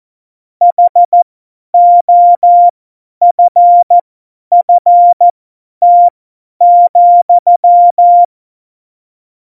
Morse code, heard as a keyed tone, about 14 wpm, spaced as for 10 wpm, 700 Hz. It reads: HOFFT,
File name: Morsekurs / Witze / Witz 17. November 2020, 50 F70